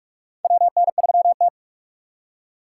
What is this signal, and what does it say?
Morse code, heard as a keyed tone, about 46 wpm, 700 Hz. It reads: WN3T
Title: USA Call Signs - 3x Character Spacing - 45wpm